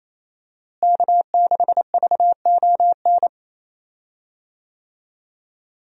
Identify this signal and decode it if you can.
Morse code, heard as a keyed tone, about 28 wpm, 700 Hz. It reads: K6VON
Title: USA Call Signs - 5 Characters - Repeated 3x - 28wpm